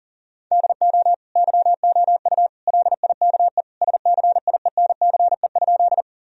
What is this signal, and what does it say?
Morse code, heard as a keyed tone, about 40 wpm, 700 Hz. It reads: DO YOU LIKE SCIENCE?
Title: Sentences from Top 1000 Words - Review Part 2 - 40wpm